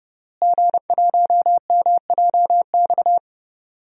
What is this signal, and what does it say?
Morse code, heard as a keyed tone, about 30 wpm, 700 Hz. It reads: G1MJX
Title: DXCC-100 Call Signs - 5 Characters - Repeated 3x - 30wpm